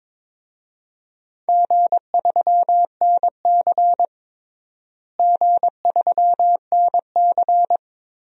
Morse code, heard as a keyed tone, about 22 wpm, 700 Hz. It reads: G3NC G3NC
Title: DXCC-100 Call Signs - 4 Characters - Repeated 3x - 22wpm